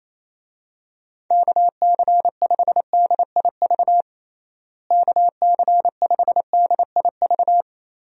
Morse code, heard as a keyed tone, about 27 wpm, 700 Hz. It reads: KC5DIV KC5DIV
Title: USA Call Signs - 6 Characters - Repeated 2x - 28wpm